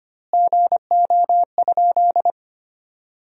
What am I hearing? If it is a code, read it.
Morse code, heard as a keyed tone, about 25 wpm, 700 Hz. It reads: GO?
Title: 2x Word-Spacing - Sentences from Top 300 Words - 25wpm